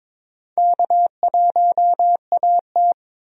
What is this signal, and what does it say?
Morse code, heard as a keyed tone, about 21 wpm, 700 Hz. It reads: K1AT
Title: USA Call Signs - 4 Characters - Repeated 3x - 22wpm